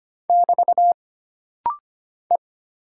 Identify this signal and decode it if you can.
Morse code, heard as a keyed tone, about 25 wpm, 700 Hz. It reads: <BT> E
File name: Single Character (Letter:Number:Punctuation:Prosign) - Rapid-Fire - 25wpm